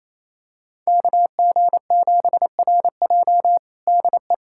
Morse code, heard as a keyed tone, about 28 wpm, 700 Hz. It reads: KG7RJ DE
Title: USA Callsign DE Callsign - 5 Characters - 28wpm